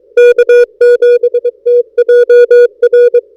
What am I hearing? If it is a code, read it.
Morse code, heard as a keyed tone, about 22 wpm, 480 Hz. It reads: K7TJR
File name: n6tr / A45XR-K7TJR-14Dec2012